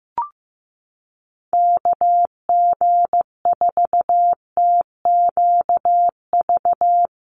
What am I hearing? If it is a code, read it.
Morse code, heard as a keyed tone, about 15 wpm, 700 Hz. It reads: KG4TQV
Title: USA Call Signs 15wpm